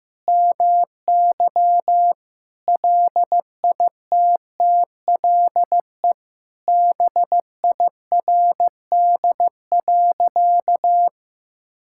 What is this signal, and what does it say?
Morse code, heard as a keyed tone, about 15 wpm, 700 Hz. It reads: MY LITTLE BIRD.
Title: Sentences from Top 1000 Words - Review Part 6 - 15wpm